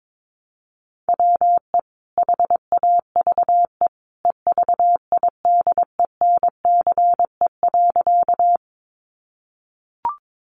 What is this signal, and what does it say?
Morse code, heard as a keyed tone, about 22 wpm, 700 Hz. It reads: WE HAVE EVIDENCE.
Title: Sentences from Top 1300 Words 22wpm